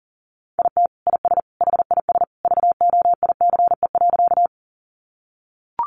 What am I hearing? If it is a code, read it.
Morse code, heard as a keyed tone, about 40 wpm, 700 Hz. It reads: IT IS HIS VOICE.
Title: Sentences from Top 500 Words 40wpm